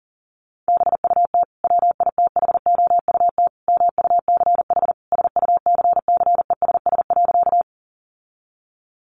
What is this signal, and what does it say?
Morse code, heard as a keyed tone, about 40 wpm, 700 Hz. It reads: BUT WITHOUT MUCH SUCCESS.
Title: Sentences from Top 900 Words 40wpm